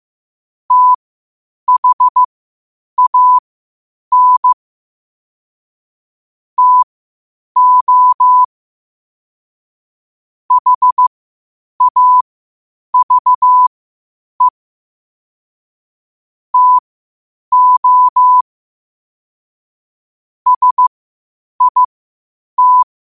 Morse code, heard as a keyed tone, about 15 wpm, spaced as for 8 wpm, 1000 Hz. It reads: THAN TO HAVE TO SIT
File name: qotdfiles / 8wpm